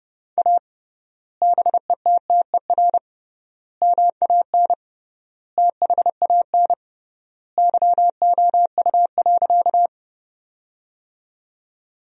Morse code, heard as a keyed tone, about 30 wpm, 700 Hz. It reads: A BETTER MAN THAN YOU.
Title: 3x Word-Spacing - Sentences from Top 400 Words - 30wpm